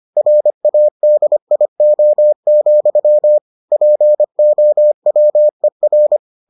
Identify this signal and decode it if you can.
Morse code, heard as a keyed tone, about 25 wpm, 600 Hz. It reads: RADIO, POWER